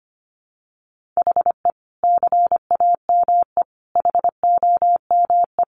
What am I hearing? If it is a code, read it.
Morse code, heard as a keyed tone, about 25 wpm, 700 Hz. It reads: HE CAME HOME
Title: Sentences from Top 1300 Words 25wpm